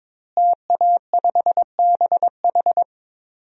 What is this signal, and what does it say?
Morse code, heard as a keyed tone, about 22 wpm, 700 Hz. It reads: TA5BH
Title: DXCC-100 Call Signs - 5 Characters - Repeated 2x - 22wpm